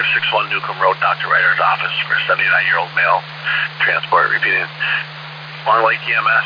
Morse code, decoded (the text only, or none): none